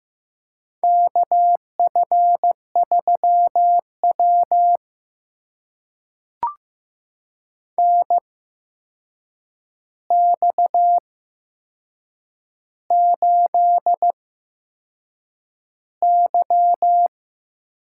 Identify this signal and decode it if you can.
Morse code, heard as a keyed tone, about 15 wpm, spaced as for 6 wpm, 700 Hz. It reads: KF3W NX8Y
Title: USA Call Signs - 4 Characters - 8x Character Spacing - 15wpm